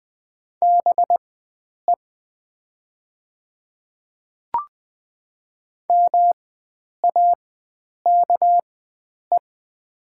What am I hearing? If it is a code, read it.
Morse code, heard as a keyed tone, about 20 wpm, spaced as for 9 wpm, 700 Hz. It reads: BE MAKE
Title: 4x Character Spacing - Top 100 Words - 20wpm